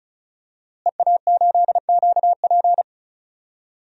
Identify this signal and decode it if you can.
Morse code, heard as a keyed tone, about 36 wpm, 700 Hz. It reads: EA8QP
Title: DXCC-100 Call Signs - 5 Characters - Repeated 3x - 35wpm